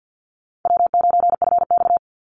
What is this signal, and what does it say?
Morse code, heard as a keyed tone, about 50 wpm, 700 Hz. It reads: W9FX
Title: Easy Call Signs Repeated 2x - 50wpm